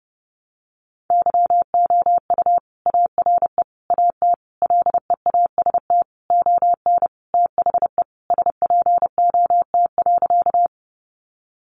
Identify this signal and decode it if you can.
Morse code, heard as a keyed tone, about 30 wpm, 700 Hz. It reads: YOU ARE AT LEAST ON THE SPOT.
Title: Sentences from Top 1100 Words 30wpm